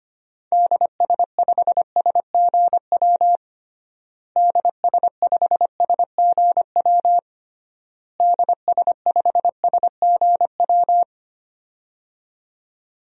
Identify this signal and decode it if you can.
Morse code, heard as a keyed tone, about 25 wpm, 700 Hz. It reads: DS5SGW DS5SGW DS5SGW
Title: DXCC-100 Call Signs - 6 Characters - Repeated 3x - 25wpm